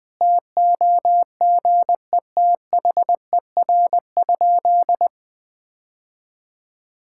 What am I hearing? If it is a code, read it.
Morse code, heard as a keyed tone, about 20 wpm, 700 Hz. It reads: TOGETHER?
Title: Sentences from Top 1600 Words 20wpm